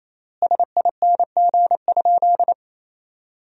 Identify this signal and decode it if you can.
Morse code, heard as a keyed tone, about 28 wpm, 700 Hz. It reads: SING?
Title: 4x Word-Spacing - Sentences from Top 400 Words - 28wpm